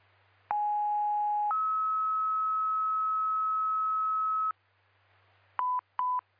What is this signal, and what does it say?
Two-tone page, heard as a one-second tone, then a three-second tone, about 830 then 1290 Hz. It is heard from 0.5 to 4.5 s.